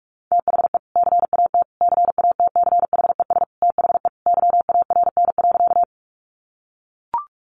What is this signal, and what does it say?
Morse code, heard as a keyed tone, about 44 wpm, 700 Hz. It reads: THE CAT CATCHES THE YARN.
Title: Additional Easy Phrases 45wpm